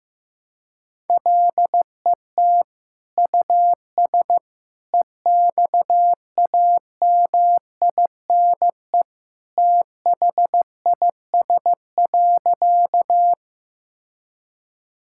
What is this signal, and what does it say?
Morse code, heard as a keyed tone, about 15 wpm, 700 Hz. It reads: LET US EXAMINE THIS.
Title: Sentences from Top 1600 Words 15wpm